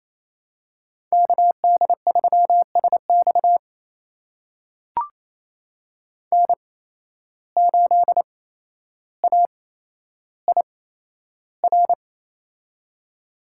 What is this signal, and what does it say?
Morse code, heard as a keyed tone, about 28 wpm, spaced as for 12 wpm, 700 Hz. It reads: KD3SX N8AIR
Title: USA Call Signs - 5 Characters - 8x Character Spacing - 28wpm